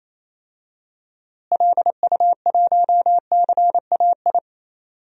Morse code, heard as a keyed tone, about 27 wpm, 700 Hz. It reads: LU1CAI
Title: DXCC-100 Call Signs - 6 Characters - Repeated 3x - 28wpm